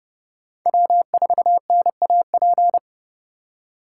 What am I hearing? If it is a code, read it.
Morse code, heard as a keyed tone, about 28 wpm, 700 Hz. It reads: W4NAP